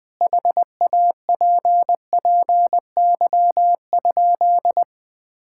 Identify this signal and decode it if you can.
Morse code, heard as a keyed tone, about 20 wpm, 700 Hz. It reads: HAPPY?
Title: Sentences from Top 1000 Words - Review Part 6 - 20wpm